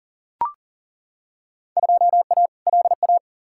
Morse code, heard as a keyed tone, about 40 wpm, 700 Hz. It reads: 2A LA
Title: Field Day Exchanges - 40wpm